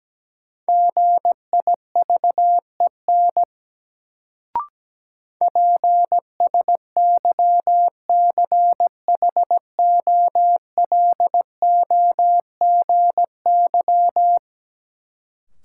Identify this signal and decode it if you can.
Morse code, heard as a keyed tone, about 17 wpm, 700 Hz. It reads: GIVEN PSYCHOLOGY
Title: Sentences from Top 1400 Words - New Word Drill Alt - 17wpm